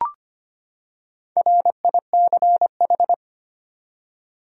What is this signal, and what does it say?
Morse code, heard as a keyed tone, about 25 wpm, 700 Hz. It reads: RICH
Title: USA Common Names - Max 5 Letters - 25wpm